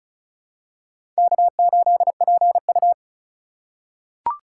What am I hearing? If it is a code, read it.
Morse code, heard as a keyed tone, about 34 wpm, 700 Hz. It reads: K8PU
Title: USA Call Signs - 4 Characters - 3x Character Spacing - 35wpm